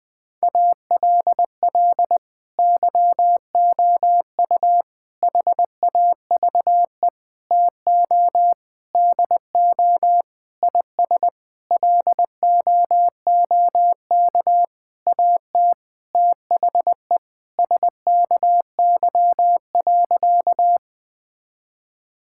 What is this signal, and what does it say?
Morse code, heard as a keyed tone, about 20 wpm, 700 Hz. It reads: ALL YOU HAVE TO DO IS LOOK AT THE SKY.
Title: Sentences with 9-Word Phrases - 20wpm